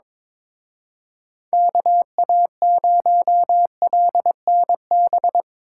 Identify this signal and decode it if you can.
Morse code, heard as a keyed tone, about 25 wpm, 700 Hz. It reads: KA0LNB
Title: Speed Racing - USA Call Signs 22wpm